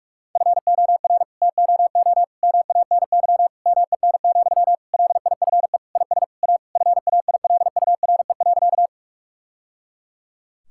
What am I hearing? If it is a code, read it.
Morse code, heard as a keyed tone, about 46 wpm, 700 Hz. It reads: FOR TOO MANY MEN, LIFE IS A FAILURE.
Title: Sentences from Top 1000 Words - Review Part 6 - 45wpm